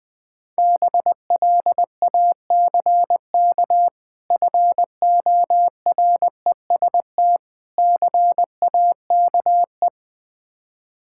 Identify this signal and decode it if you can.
Morse code, heard as a keyed tone, about 20 wpm, 700 Hz. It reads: BLACK FOREST CAKE
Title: Difficult Words with Hints - 20wpm